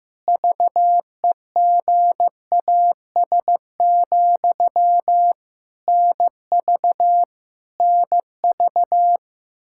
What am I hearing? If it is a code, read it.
Morse code, heard as a keyed tone, about 15 wpm, 700 Hz. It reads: VEGAS, NV NV
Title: K1USN SST QSO Practice 15wpm